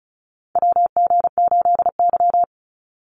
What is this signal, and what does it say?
Morse code, heard as a keyed tone, about 36 wpm, 700 Hz. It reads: WG8Y
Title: NA SOTA Chaser Callsigns 35wpm